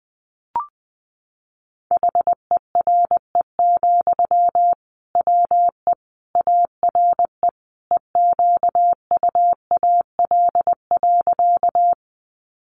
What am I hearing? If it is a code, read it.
Morse code, heard as a keyed tone, about 20 wpm, 700 Hz. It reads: HERE, WE ARE EQUAL.
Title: Sentences from Top 1400 Words 20wpm